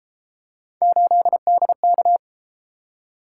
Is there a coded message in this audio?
Yes: Morse code from 0.8 to 2.2 s, heard as a keyed tone, about 33 wpm, 700 Hz. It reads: 8DK